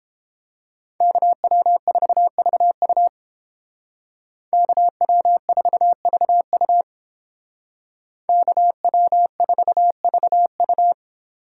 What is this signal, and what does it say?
Morse code, heard as a keyed tone, about 31 wpm, 700 Hz. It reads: KW4VU KW4VU KW4VU